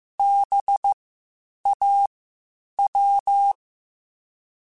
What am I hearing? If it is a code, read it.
Morse code, heard as a keyed tone, about 15 wpm, spaced as for 8 wpm, 800 Hz. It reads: BAW